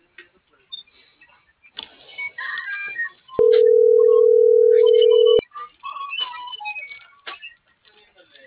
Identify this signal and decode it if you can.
Ringback tone (North America), 440 + 480 Hz.